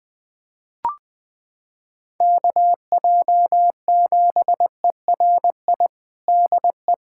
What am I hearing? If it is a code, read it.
Morse code, heard as a keyed tone, about 20 wpm, 700 Hz. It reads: KJ7ERI DE
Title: USA Callsign DE Callsign 20wpm